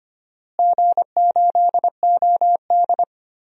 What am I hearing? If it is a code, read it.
Morse code, heard as a keyed tone, about 25 wpm, 700 Hz. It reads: G8OD
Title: DXCC-100 Call Signs - 4 Characters - Repeated 3x - 25wpm